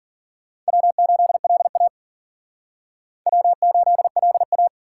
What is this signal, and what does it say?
Morse code, heard as a keyed tone, about 52 wpm, 700 Hz. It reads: W8LA W8LA